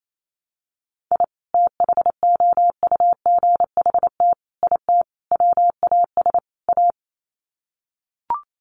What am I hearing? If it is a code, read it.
Morse code, heard as a keyed tone, about 28 wpm, 700 Hz. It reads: I THOUGHT IT WAS A
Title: Sets of 5 Words 28wpm